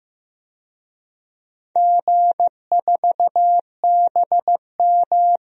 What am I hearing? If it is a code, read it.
Morse code, heard as a keyed tone, about 15 wpm, 700 Hz. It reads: G4BM
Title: DXCC-100 Call Signs - 4 Characters - 15wpm